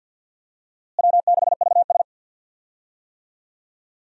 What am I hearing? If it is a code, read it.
Morse code, heard as a keyed tone, about 52 wpm, 700 Hz. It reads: W6VS
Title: USA Call Signs - 4 Characters - Repeated 3x - 50wpm